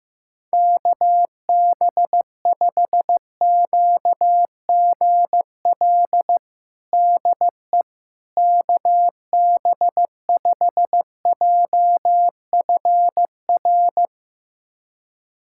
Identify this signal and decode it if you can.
Morse code, heard as a keyed tone, about 15 wpm, 700 Hz. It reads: KB5QGL DE KB5JFR
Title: USA Callsign DE Callsign - 6 Characters - 15wpm